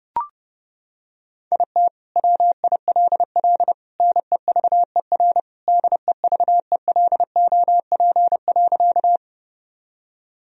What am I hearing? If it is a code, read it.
Morse code, heard as a keyed tone, about 32 wpm, 700 Hz. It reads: IT WILL NEVER DEVELOP.
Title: Sentences from Top 1000 Words - Review Part 2 - 30wpm